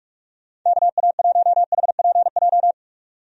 Morse code, heard as a keyed tone, about 46 wpm, 700 Hz. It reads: KA1HPJ
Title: USA Call Signs - 6 Characters - 45wpm